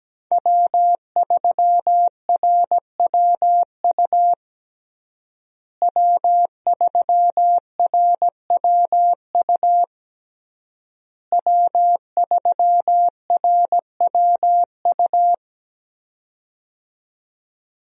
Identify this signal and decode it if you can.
Morse code, heard as a keyed tone, about 17 wpm, 700 Hz. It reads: W3RWU W3RWU W3RWU
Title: USA Call Signs - 5 Characters - Repeated 3x - 17wpm